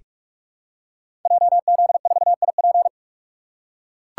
Morse code, heard as a keyed tone, about 44 wpm, 700 Hz. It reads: JZ4IP
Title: Code Groups - 5 Letter:Numbers - 45wpm